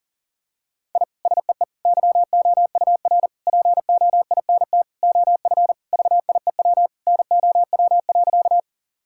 Morse code, heard as a keyed tone, about 40 wpm, 700 Hz. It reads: I SEE YOUR POINT OF VIEW NOW.
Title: Sentences from Top 1000 Words - Review Part 6 - 40wpm